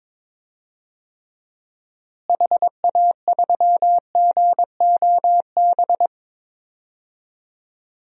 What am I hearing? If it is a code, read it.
Morse code, heard as a keyed tone, about 22 wpm, 700 Hz. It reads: HA3GOB
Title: DXCC-100 Call Signs - 6 Characters - Repeated 3x - 22wpm